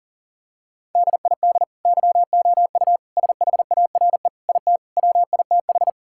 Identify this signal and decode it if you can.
Morse code, heard as a keyed tone, about 40 wpm, 700 Hz. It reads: DID YOU SHARE IT WITH
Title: Sentences from Top 1000 Words - Review Part 6 - 40wpm